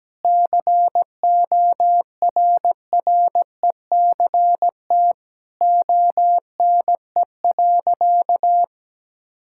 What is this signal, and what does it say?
Morse code, heard as a keyed tone, about 17 wpm, 700 Hz. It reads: CORRECT ONE.
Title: Sentences from Top 1000 Words - Review Part 6 - 17wpm